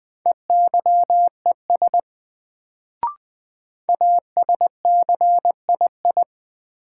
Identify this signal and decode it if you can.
Morse code, heard as a keyed tone, about 20 wpm, 700 Hz. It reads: EYES ASCII
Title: Sets of 5 Words - New Word Drill - 20wpm